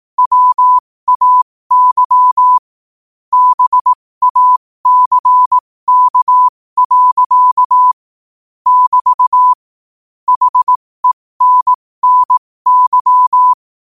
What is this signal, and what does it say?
Morse code, heard as a keyed tone, about 18 wpm, spaced as for 15 wpm, 1000 Hz. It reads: WAY BACK. <BT> HENNY
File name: qotdfiles / 15wpm